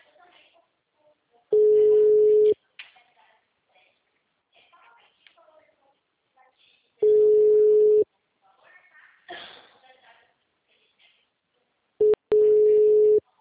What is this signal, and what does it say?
Ringback tone (Europe, 425 Hz), with speech.